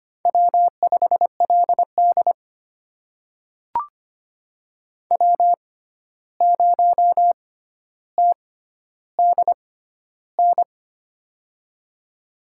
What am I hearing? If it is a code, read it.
Morse code, heard as a keyed tone, about 25 wpm, spaced as for 12 wpm, 700 Hz. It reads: W5LD W0TDN
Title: USA Call Signs - 6x Character Spacing - 25wpm